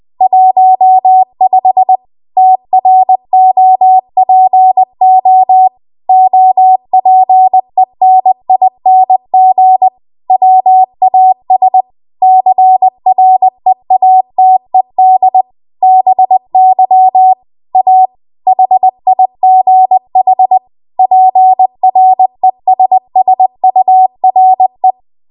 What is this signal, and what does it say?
Morse code, heard as a keyed tone, about 20 wpm, 750 Hz. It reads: 15 TROPO OPENING WAS CREATED BY A HIGH PRESSURE
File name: Archive / 20 WPM / 20WPM